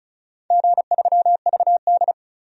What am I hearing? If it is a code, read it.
Morse code, heard as a keyed tone, about 34 wpm, 700 Hz. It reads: G3VD
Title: DXCC-100 Call Signs - 4 Characters - 35wpm